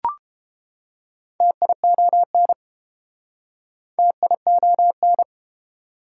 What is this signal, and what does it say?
Morse code, heard as a keyed tone, about 28 wpm, 700 Hz. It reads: TION TION